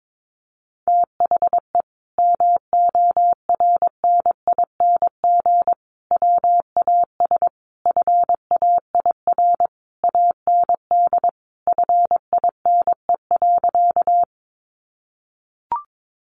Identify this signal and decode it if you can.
Morse code, heard as a keyed tone, about 22 wpm, 700 Hz. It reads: THE MORNING WAS FAIR AND FINE.
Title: Sentences from Top 1300 Words 22wpm